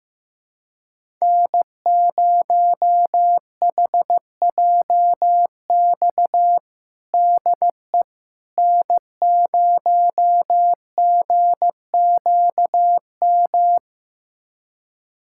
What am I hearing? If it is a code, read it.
Morse code, heard as a keyed tone, about 15 wpm, 700 Hz. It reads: N0HJX DE N0GQM
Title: USA Callsign DE Callsign - 5 Characters - 15wpm